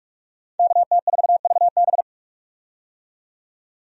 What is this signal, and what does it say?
Morse code, heard as a keyed tone, about 46 wpm, 700 Hz. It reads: KT4VB
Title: USA Call Signs - 5 Characters - 45wpm